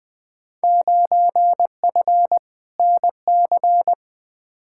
Morse code, heard as a keyed tone, about 20 wpm, 700 Hz. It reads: 9F NC
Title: Field Day Exchanges Encore - 20wpm